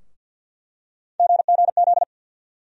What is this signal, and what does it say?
Morse code, heard as a keyed tone, about 47 wpm, 700 Hz. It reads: GGZ
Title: USA Call Sign Postfixes - 3 Letters 50wpm